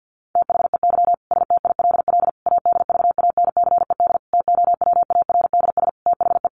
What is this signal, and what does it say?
Morse code, heard as a keyed tone, about 50 wpm, 700 Hz. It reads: THEY STILL ADVANCED TOWARDS THE
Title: Sentences from Top 1200 Words 50wpm